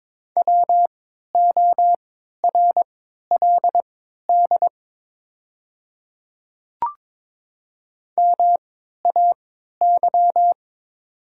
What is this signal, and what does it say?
Morse code, heard as a keyed tone, about 22 wpm, spaced as for 12 wpm, 700 Hz. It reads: WORLD MAY
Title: 3x Character Spacing - Top 300 Words - 22wpm